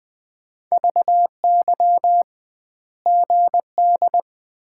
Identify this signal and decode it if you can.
Morse code, heard as a keyed tone, about 20 wpm, 700 Hz. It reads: VY GD
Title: Common QSO Phrases - 2x Word-Spacing - 20wpm